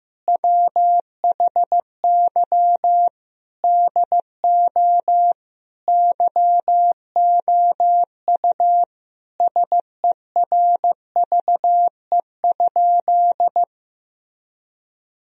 Morse code, heard as a keyed tone, about 15 wpm, 700 Hz. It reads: WHY DO YOU SERVE?
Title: Sentences from Top 1000 Words - Review Part 2 - 15wpm